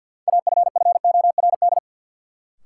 Morse code, heard as a keyed tone, about 52 wpm, 700 Hz. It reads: AFFORD
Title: Sentences from Top 1100 Words - New Word Drill Alt - 50wpm